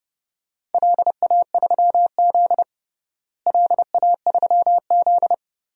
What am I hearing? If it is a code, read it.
Morse code, heard as a keyed tone, about 32 wpm, 700 Hz. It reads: LA3Z LA3Z
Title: DXCC-100 Call Signs - 4 Characters - Repeated 3x - 30wpm